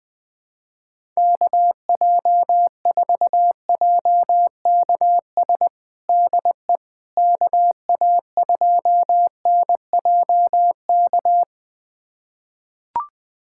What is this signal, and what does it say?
Morse code, heard as a keyed tone, about 21 wpm, 700 Hz. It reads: KJ4JKS DE KA2NJK
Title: USA Callsign DE Callsign - 6 Characters - 20wpm